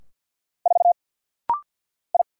Morse code, heard as a keyed tone, about 50 wpm, 690 Hz. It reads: 4 I
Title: Single Letter-Number Alt - Rapid-Fire 50wpm